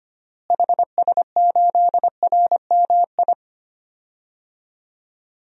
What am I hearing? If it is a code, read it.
Morse code, heard as a keyed tone, about 25 wpm, 700 Hz. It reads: HS8RMI
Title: DXCC-100 Call Signs - 6 Characters - Repeated 3x - 25wpm